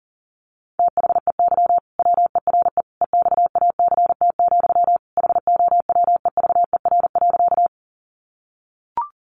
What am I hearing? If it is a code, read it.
Morse code, heard as a keyed tone, about 40 wpm, 700 Hz. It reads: THEY WERE EXACT, HOWEVER.
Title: Sentences from Top 1300 Words 40wpm